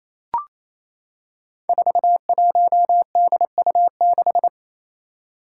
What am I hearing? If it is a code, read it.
Morse code, heard as a keyed tone, about 28 wpm, 700 Hz. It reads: 41DU6
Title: Five Challenging Letter-Number Combinations 28wpm